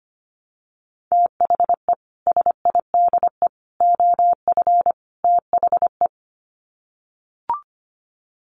Morse code, heard as a keyed tone, about 25 wpm, 700 Hz. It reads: THE SIDE OF THE
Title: Sets of 4 Words 25wpm